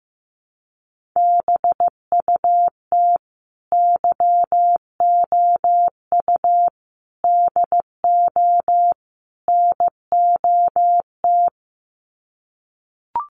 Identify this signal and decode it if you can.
Morse code, heard as a keyed tone, about 15 wpm, 700 Hz. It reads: BUT YOU DO NOT
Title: Sets of 4 Words 15wpm